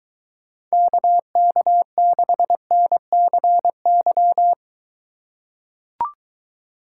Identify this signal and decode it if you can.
Morse code, heard as a keyed tone, about 23 wpm, 700 Hz. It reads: KK6NCY